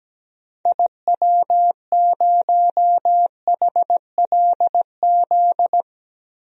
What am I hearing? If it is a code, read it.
Morse code, heard as a keyed tone, about 17 wpm, 700 Hz. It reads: IW0HLZ
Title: DXCC-100 Call Signs - 6 Characters - 17wpm